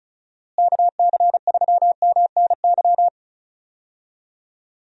Morse code, heard as a keyed tone, about 36 wpm, 700 Hz. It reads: KC3MNY
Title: USA Call Signs - 6 Characters - Repeated 2x - 35wpm